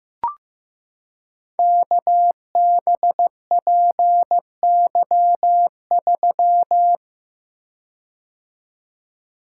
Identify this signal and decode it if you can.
Morse code, heard as a keyed tone, about 15 wpm, 700 Hz. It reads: KBPY3
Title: Code Groups - 5 Letter:Numbers - 15wpm